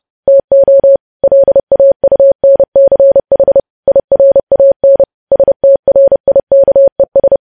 Morse code, heard as a keyed tone, about 30 wpm, 550 Hz. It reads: TO LAUNCH IRAN STRIKES
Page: news headlines in morse code at 30 wpm